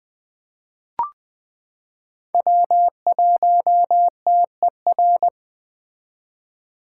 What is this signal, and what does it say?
Morse code, heard as a keyed tone, about 20 wpm, 700 Hz. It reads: W1TER
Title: K1USN SST QSO Practice - 20wpm